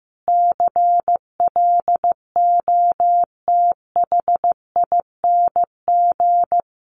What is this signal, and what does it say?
Morse code, heard as a keyed tone, about 15 wpm, 700 Hz. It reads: CLOTHING
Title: A Sound Of Thunder 15wpm